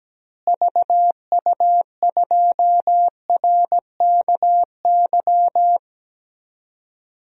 Morse code, heard as a keyed tone, about 17 wpm, 700 Hz. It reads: VU2RKY
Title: DXCC-100 Call Signs - 6 Characters - Repeated 3x - 17wpm